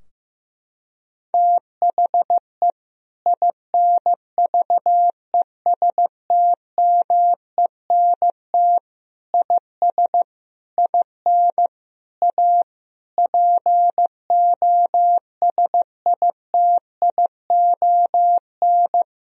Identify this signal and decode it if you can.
Morse code, heard as a keyed tone, about 16 wpm, 700 Hz. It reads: THE INVESTMENT IS IN A POSITION
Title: Sentences from Top 1000 Words - Review Part 4 - 15wpm